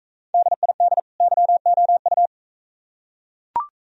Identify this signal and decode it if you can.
Morse code, heard as a keyed tone, about 44 wpm, 700 Hz. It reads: DID YOU